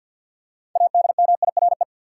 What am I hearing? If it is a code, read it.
Morse code, heard as a keyed tone, about 50 wpm, 700 Hz. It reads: ADMIRE
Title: Six Letter Words - 50wpm